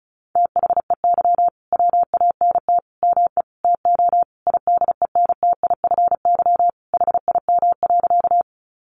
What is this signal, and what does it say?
Morse code, heard as a keyed tone, about 34 wpm, 700 Hz. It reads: THEY WANT ME TO IDENTIFY HIM.
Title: Sentences from Top 900 Words 35wpm